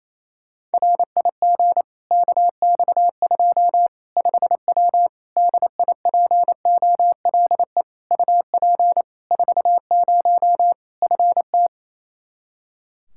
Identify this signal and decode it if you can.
Morse code, heard as a keyed tone, about 28 wpm, 700 Hz. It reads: RIG KX2 5W DIPOLE UP 40 FT
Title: Practice QSO - Exchange 2 - Info - 28wpm